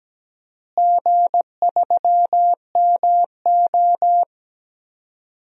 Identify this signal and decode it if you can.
Morse code, heard as a keyed tone, about 17 wpm, 700 Hz. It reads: G3MO
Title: DXCC-100 Call Signs - 4 Characters - Repeated 3x - 17wpm